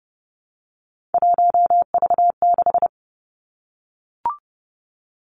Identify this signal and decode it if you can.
Morse code, heard as a keyed tone, about 30 wpm, 700 Hz. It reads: 1V6
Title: Three Challenging Letter-Number Combinations 30wpm